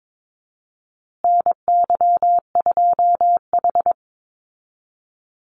Morse code, heard as a keyed tone, about 22 wpm, 700 Hz. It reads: NY2H
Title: Call Signs Repeated 2x - 22wpm